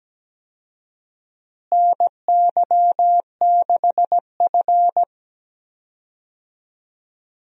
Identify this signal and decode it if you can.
Morse code, heard as a keyed tone, about 17 wpm, 700 Hz. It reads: NY6F
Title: USA Call Signs - 4 Characters - Repeated 3x - 17wpm